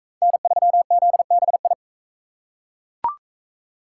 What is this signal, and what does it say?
Morse code, heard as a keyed tone, about 37 wpm, 700 Hz. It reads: N3ZBI